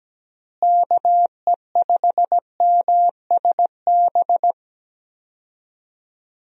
Morse code, heard as a keyed tone, about 17 wpm, 700 Hz. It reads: KE5MSB
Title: USA Call Signs - 6 Characters - Repeated 3x - 17wpm